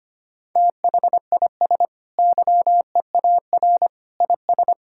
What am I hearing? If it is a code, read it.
Morse code, heard as a keyed tone, about 25 wpm, 700 Hz. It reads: THIS YEAR IS
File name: practice / 25wpm / Sentences with 2-Word Phrases - 25wpm